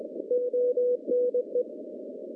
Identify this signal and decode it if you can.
Morse code, heard as a keyed tone, about 21 wpm, 490 Hz. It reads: ONE
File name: contests / CW-SS-k5tr- -0400